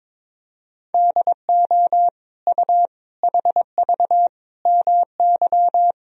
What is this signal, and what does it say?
Morse code, heard as a keyed tone, about 22 wpm, 700 Hz. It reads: DO U HV MY
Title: Sentences from Top 500 Words and Abbreviations 22wpm